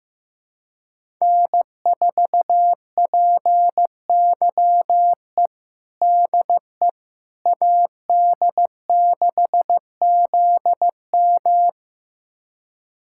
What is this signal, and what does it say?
Morse code, heard as a keyed tone, about 15 wpm, 700 Hz. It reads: N4PYE DE AD6ZM
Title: USA Callsign DE Callsign - 5 Characters - 15wpm